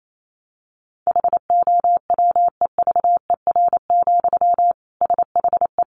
Morse code, heard as a keyed tone, about 28 wpm, 700 Hz. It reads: HOWEVER, SHE
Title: Sentences from Top 600 Words 28wpm